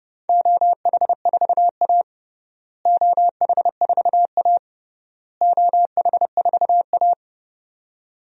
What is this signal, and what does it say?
Morse code, heard as a keyed tone, about 32 wpm, 700 Hz. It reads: OH4A OH4A OH4A
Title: DXCC-100 Call Signs - 4 Characters - Repeated 3x - 30wpm